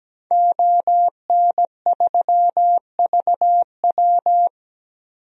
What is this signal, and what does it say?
Morse code, heard as a keyed tone, about 17 wpm, 700 Hz. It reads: ON3VW
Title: DXCC-100 Call Signs - 5 Characters - 17wpm